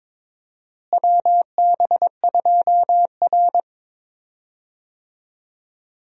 Morse code, heard as a keyed tone, about 21 wpm, 700 Hz. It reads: WB2R
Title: USA Call Signs - 4 Characters - Repeated 2x - 22wpm